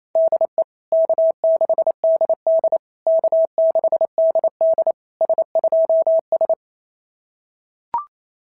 Morse code, heard as a keyed tone, about 28 wpm, 690 Hz. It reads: DE K6DD K6DD S2S
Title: SOTA QSO Practice 28wpm